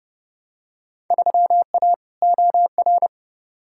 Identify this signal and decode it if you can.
Morse code, heard as a keyed tone, about 30 wpm, 700 Hz. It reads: 3A OR
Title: Field Day Exchanges - 30wpm